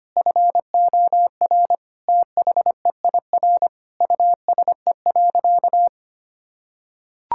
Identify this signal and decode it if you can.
Morse code, heard as a keyed tone, about 25 wpm, 700 Hz. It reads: FOR THEIR USE.
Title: Sentences from Top 1000 Words - Review Part 6 - 25wpm